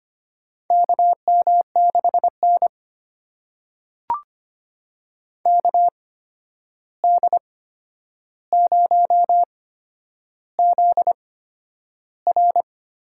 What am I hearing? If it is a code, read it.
Morse code, heard as a keyed tone, about 25 wpm, spaced as for 10 wpm, 700 Hz. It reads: KM6N KD0ZR
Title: USA Call Signs - 8x Character Spacing - 25wpm